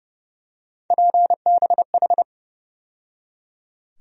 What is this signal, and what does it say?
Morse code, heard as a keyed tone, about 30 wpm, 700 Hz. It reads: PBH
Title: USA Call Sign Postfixes - 3 Letters 30wpm